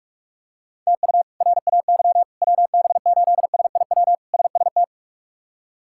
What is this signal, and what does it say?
Morse code, heard as a keyed tone, about 41 wpm, 700 Hz. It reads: TU RAY WB8SIW SST
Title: K1USN SST QSO Practice - 45wpm